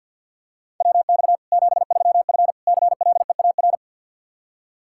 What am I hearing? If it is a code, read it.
Morse code, heard as a keyed tone, about 49 wpm, 700 Hz. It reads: WX 73F CLEAR
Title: Practice QSO - Exchange 2 - Info - 50wpm